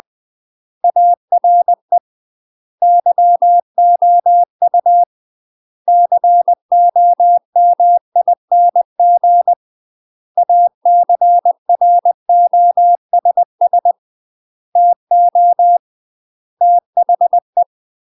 Morse code, heard as a keyed tone, about 20 wpm, 700 Hz. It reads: ARE YOU COMING ACROSS TO THE
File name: UK MASTER QSOs TRAINING / 20WPM / UK MASTER QSOs 0030